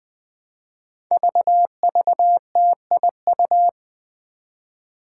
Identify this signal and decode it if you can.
Morse code, heard as a keyed tone, about 21 wpm, 700 Hz. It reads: VVTIU
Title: Five Challenging Letter-Number Combinations 20wpm